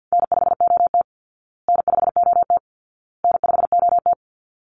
Morse code, heard as a keyed tone, about 50 wpm, 700 Hz. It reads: N5OT N5OT N5OT
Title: Easy Call Signs Repeated 3x - 50wpm